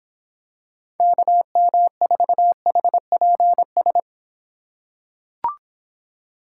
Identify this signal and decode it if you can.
Morse code, heard as a keyed tone, about 26 wpm, 700 Hz. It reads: KM4HPS